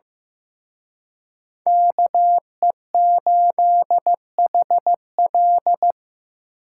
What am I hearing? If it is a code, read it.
Morse code, heard as a keyed tone, about 18 wpm, 700 Hz. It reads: KE8HL
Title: Speed Racing - USA Call Signs 15wpm